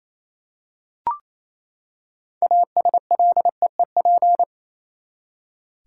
Morse code, heard as a keyed tone, about 27 wpm, 700 Hz. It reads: ASLEEP
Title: Six Letter Words - 28wpm